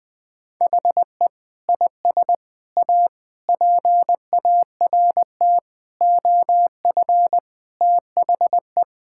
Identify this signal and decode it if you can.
Morse code, heard as a keyed tone, about 20 wpm, 700 Hz. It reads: HE IS A PART OF THE
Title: Sentences with 4-Word Phrases - Spoken Beginning - 20wpm